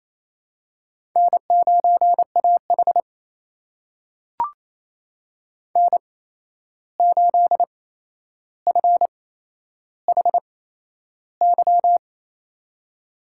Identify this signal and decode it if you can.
Morse code, heard as a keyed tone, about 28 wpm, spaced as for 11 wpm, 700 Hz. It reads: N9AH N8FHY
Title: USA Call Signs - 8x Character Spacing - 28wpm